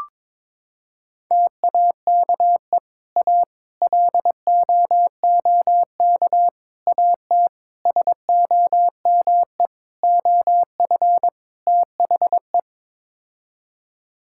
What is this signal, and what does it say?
Morse code, heard as a keyed tone, about 22 wpm, 700 Hz. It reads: TAKE A LOOK AT SOME OF THE
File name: practice / 22wpm / Sets of 7 Words - 22wpm